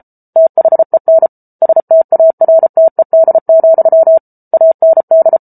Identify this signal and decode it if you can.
Morse code, heard as a keyed tone, about 34 wpm, 650 Hz. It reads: THEN STARTED, AND